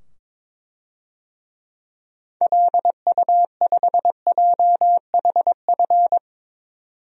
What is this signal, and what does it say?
Morse code, heard as a keyed tone, about 22 wpm, 700 Hz. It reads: LU5JHF
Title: DXCC-100 Call Signs - 6 Characters - 22wpm